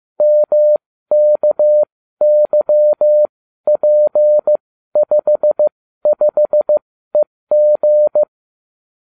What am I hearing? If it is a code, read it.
Morse code, heard as a keyed tone, about 15 wpm, spaced as for 12 wpm, 600 Hz. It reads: MKYP55EG